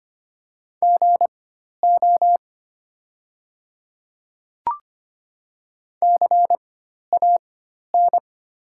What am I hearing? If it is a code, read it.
Morse code, heard as a keyed tone, about 25 wpm, spaced as for 12 wpm, 700 Hz. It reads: GO CAN
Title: 4x Character Spacing - Top 100 Words - 25wpm